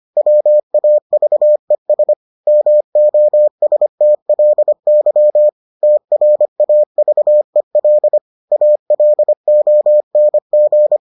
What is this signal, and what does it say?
Morse code, heard as a keyed tone, about 25 wpm, 600 Hz. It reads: WAVES MOSTLY TRAVEL ALONG